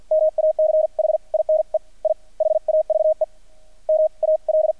Morse code, heard as a keyed tone, about 50 wpm, spaced as for 41 wpm, 630 Hz. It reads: MACHINE I HAVE MAL
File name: archived / CRSnet